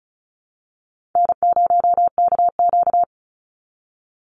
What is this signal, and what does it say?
Morse code, heard as a keyed tone, about 34 wpm, 700 Hz. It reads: N0KQ
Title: Easy Call Signs Repeated 2x - 35wpm